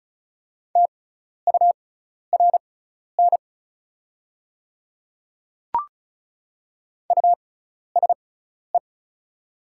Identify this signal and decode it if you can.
Morse code, heard as a keyed tone, about 36 wpm, spaced as for 12 wpm, 690 Hz. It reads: TURN USE
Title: 6x Character Spacing - Top 200 Words - 35wpm